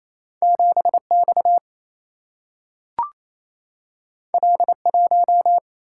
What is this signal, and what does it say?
Morse code, heard as a keyed tone, about 28 wpm, 700 Hz. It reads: 7X L1
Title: Single-Letter Single-Number 28wpm